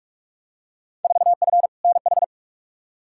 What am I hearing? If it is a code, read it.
Morse code, heard as a keyed tone, about 46 wpm, 700 Hz. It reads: 4F NH
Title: Field Day Exchanges Encore - 45wpm